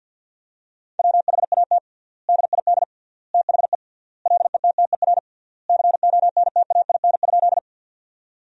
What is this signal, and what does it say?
Morse code, heard as a keyed tone, about 50 wpm, 700 Hz. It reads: WHAT DID THE LETTER CONTAIN?
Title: 3x Word-Spacing - Sentences from Top 500 Words - 50wpm